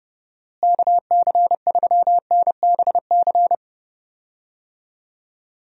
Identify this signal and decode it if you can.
Morse code, heard as a keyed tone, about 32 wpm, 700 Hz. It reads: KC3NBC
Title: USA Call Signs - 6 Characters - Repeated 2x - 30wpm